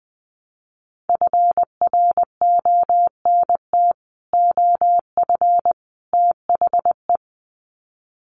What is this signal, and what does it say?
Morse code, heard as a keyed tone, about 20 wpm, 700 Hz. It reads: FRONT OF THE
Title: Sets of 3 Words 20wpm